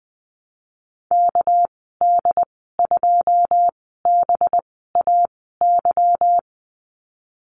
Morse code, heard as a keyed tone, about 20 wpm, spaced as for 15 wpm, 700 Hz. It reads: KD2BAY
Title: 231 Lesson 37g - Letter x - Call Signs 20wpm